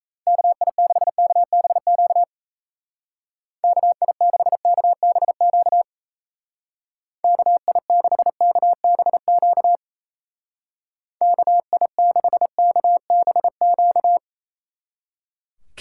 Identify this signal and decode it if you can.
Morse code, heard as a keyed tone, about 35 wpm, 700 Hz. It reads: KI6KBQ KI6KBQ KI6KBQ KI6KBQ